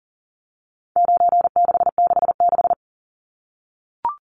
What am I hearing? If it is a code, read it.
Morse code, heard as a keyed tone, about 40 wpm, 700 Hz. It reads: 9666
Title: Numbers - 4-Digit 40wpm